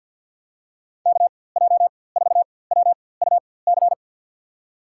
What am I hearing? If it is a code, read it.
Morse code, heard as a keyed tone, about 46 wpm, spaced as for 23 wpm, 700 Hz. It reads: KJ4WUC
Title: USA Call Signs - 4x Character Spacing - 50wpm